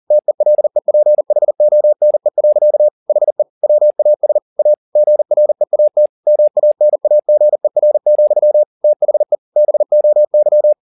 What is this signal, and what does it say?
Morse code, heard as a keyed tone, about 40 wpm, 600 Hz. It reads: TELEPHONE. HE WAS A GREAT MANAGER, THE BOY